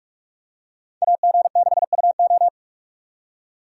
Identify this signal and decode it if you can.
Morse code, heard as a keyed tone, about 44 wpm, 700 Hz. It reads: AG6UO